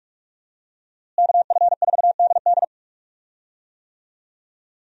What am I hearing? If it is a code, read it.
Morse code, heard as a keyed tone, about 47 wpm, 700 Hz. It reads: KF4DD
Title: USA Call Signs - 5 Characters - Repeated 3x - 45wpm